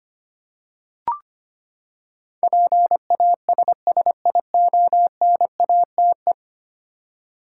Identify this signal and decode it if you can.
Morse code, heard as a keyed tone, about 25 wpm, 700 Hz. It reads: PASSIONATE
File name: Ten Letter Words - 25wpm